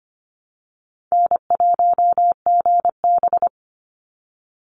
Morse code, heard as a keyed tone, about 25 wpm, 700 Hz. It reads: N1GB
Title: NA SOTA Chaser Callsigns 25wpm